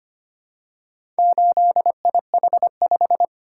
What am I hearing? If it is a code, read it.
Morse code, heard as a keyed tone, about 25 wpm, 700 Hz. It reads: 8IH5
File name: practice / 25wpm / Four Letter-Number Combinations - 25wpm